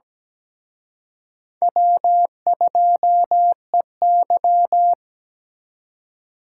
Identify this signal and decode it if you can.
Morse code, heard as a keyed tone, about 20 wpm, 700 Hz. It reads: W2EY
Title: Speed Racing - USA Call Signs 17wpm